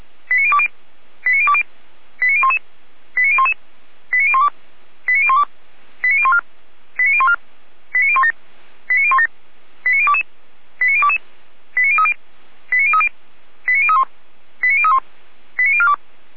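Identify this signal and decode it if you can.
Five-tone selective call (ZVEI), heard as quick runs of five tones, tones of about 80 ms. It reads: 89020 89020 89011 89011 89012 89012 89014 89014 89017 89017 89022 89022 89030 89030 89031 89031 89042
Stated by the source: ZVEI five-tone selective call received with an SDR, from fire brigade Feuerwehr Orlamünde.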